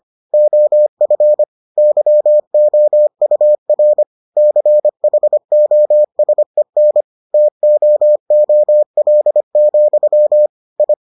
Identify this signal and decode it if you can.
Morse code, heard as a keyed tone, about 25 wpm, 600 Hz. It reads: OF YOUR CHOSEN TOOL, I